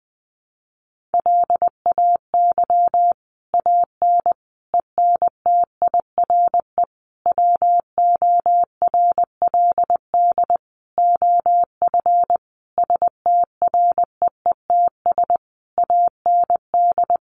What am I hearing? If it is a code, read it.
Morse code, heard as a keyed tone, about 20 wpm, 700 Hz. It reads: LAY AN ENTIRE WORLD OF STREETS AND
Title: A Sound Of Thunder 20wpm